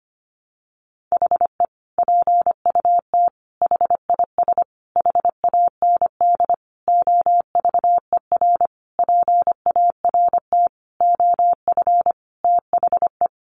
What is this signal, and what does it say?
Morse code, heard as a keyed tone, about 25 wpm, 700 Hz. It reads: HE PUT HIS HAND OVER PART OF THE
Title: Sentences from Top 400 Words 25wpm